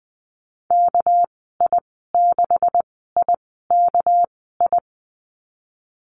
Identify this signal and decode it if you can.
Morse code, heard as a keyed tone, about 20 wpm, spaced as for 14 wpm, 700 Hz. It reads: KI6IKI
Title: 200 Lesson 32g - Letter k - Call Signs 20wpm